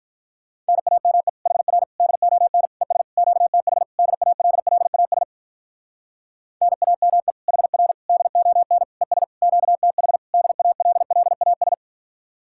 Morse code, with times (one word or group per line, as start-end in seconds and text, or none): 0.7-1.3 NAME
1.5-1.8 HR
2.0-2.7 DON
2.8-3.0 ES
3.2-3.8 QTH
4.0-5.2 DALLAS
6.6-7.3 NAME
7.5-7.9 HR
8.1-8.8 DON
9.0-9.2 ES
9.4-10.2 QTH
10.3-11.7 DALLAS